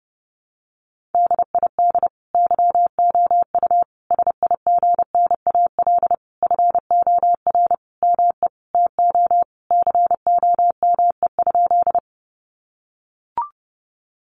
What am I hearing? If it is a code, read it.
Morse code, heard as a keyed tone, about 30 wpm, 700 Hz. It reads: DID YOU SIGNAL FOR ME TO COME?
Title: Sentences from Top 1400 Words 30wpm